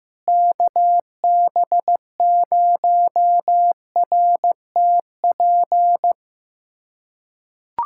Morse code, heard as a keyed tone, about 16 wpm, 700 Hz. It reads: KB0RTP
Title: USA Callsign DE Callsign - 6 Characters - 15wpm